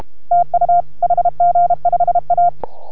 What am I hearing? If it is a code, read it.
Morse code, heard as a keyed tone, about 32 wpm, 690 Hz. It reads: TU HG5A